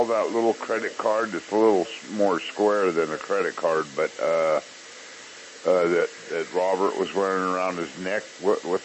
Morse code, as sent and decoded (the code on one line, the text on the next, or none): none